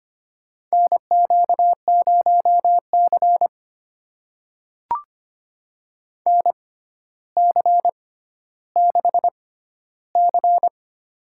Morse code, heard as a keyed tone, about 25 wpm, spaced as for 13 wpm, 700 Hz. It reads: NQ0C NC6C
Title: USA Call Signs - 4 Characters - 6x Character Spacing - 25wpm